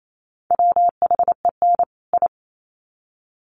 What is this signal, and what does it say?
Morse code, heard as a keyed tone, about 28 wpm, 700 Hz. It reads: WHEN I
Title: Sets of 2 Words 28wpm